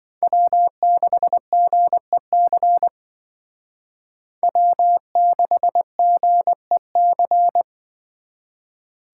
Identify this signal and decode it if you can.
Morse code, heard as a keyed tone, about 24 wpm, 700 Hz. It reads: W6GEC W6GEC
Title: Speed Racing - USA 1x3 Call Signs 20wpm